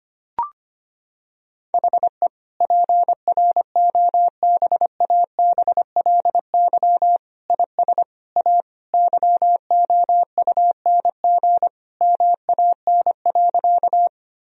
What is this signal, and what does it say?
Morse code, heard as a keyed tone, about 25 wpm, 700 Hz. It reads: HE PROBABLY IS A YOUNG MAN.
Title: Sentences from Top 1000 Words - Review Part 4 - 25wpm